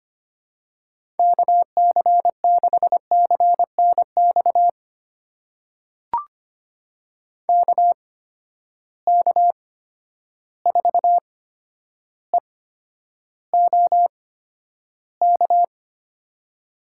Morse code, heard as a keyed tone, about 25 wpm, spaced as for 11 wpm, 700 Hz. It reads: KC6CNX KK4EOK
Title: USA Call Signs - 6 Characters - 8x Character Spacing - 25wpm